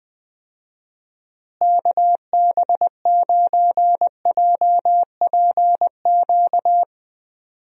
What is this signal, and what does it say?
Morse code, heard as a keyed tone, about 20 wpm, 700 Hz. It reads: KB9JPQ
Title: USA Call Signs - 6 Characters - Repeated 3x - 20wpm